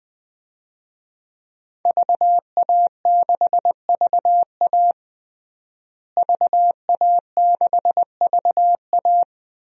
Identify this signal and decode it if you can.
Morse code, heard as a keyed tone, about 20 wpm, 700 Hz. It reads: VA6VA VA6VA
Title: DXCC-100 Call Signs - 5 Characters - Repeated 3x - 20wpm